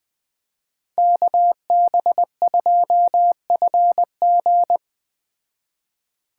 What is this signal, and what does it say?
Morse code, heard as a keyed tone, about 20 wpm, 700 Hz. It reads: KB2FG